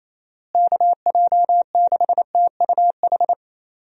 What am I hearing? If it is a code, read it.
Morse code, heard as a keyed tone, about 28 wpm, 700 Hz. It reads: KJ6TUH
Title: USA Call Signs - 6 Characters - Repeated 3x - 28wpm